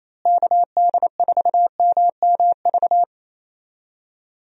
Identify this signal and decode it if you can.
Morse code, heard as a keyed tone, about 28 wpm, 700 Hz. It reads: KD4MMV
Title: USA Call Signs - 6 Characters - Repeated 3x - 28wpm